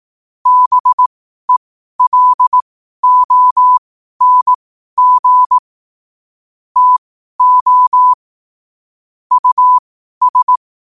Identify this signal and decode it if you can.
Morse code, heard as a keyed tone, about 18 wpm, spaced as for 12 wpm, 1000 Hz. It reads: BELONG TO US
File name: qotdfiles / 12wpm